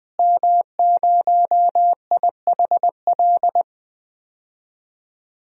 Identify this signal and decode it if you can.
Morse code, heard as a keyed tone, about 20 wpm, 700 Hz. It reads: M0IHL
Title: DXCC-100 Call Signs - 5 Characters - Repeated 3x - 20wpm